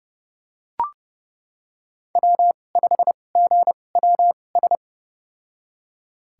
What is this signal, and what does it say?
Morse code, heard as a keyed tone, about 30 wpm, spaced as for 22 wpm, 700 Hz. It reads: W5GWS
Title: USA Call Signs - 5 Characters - 2x Character Spacing - 30wpm